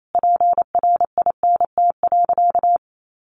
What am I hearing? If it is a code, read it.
Morse code, heard as a keyed tone, about 28 wpm, 700 Hz. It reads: PRINT.
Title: Sentences from Top 1400 Words 28wpm